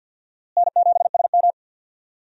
Morse code, heard as a keyed tone, about 52 wpm, 700 Hz. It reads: N7SM
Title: USA Call Signs - 4 Characters - Repeated 2x - 50wpm